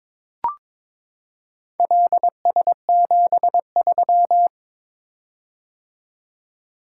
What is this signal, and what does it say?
Morse code, heard as a keyed tone, about 22 wpm, 700 Hz. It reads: LS73
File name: practice / 22wpm / Four Letter-Number Combinations - 22wpm